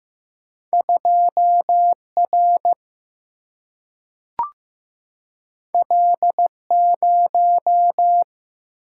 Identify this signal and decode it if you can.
Morse code, heard as a keyed tone, about 15 wpm, 700 Hz. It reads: 2R L0
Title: Single-Letter Single-Number 15wpm